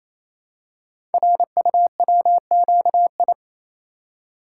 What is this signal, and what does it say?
Morse code, heard as a keyed tone, about 28 wpm, 700 Hz. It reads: RUWQI
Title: Code Groups - 5 Letters - 28wpm